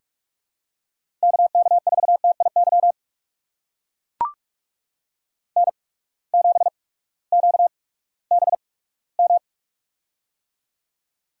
Morse code, heard as a keyed tone, about 44 wpm, spaced as for 20 wpm, 700 Hz. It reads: KK4TIY N7QBM
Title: USA Call Signs - 8x Character Spacing - 45wpm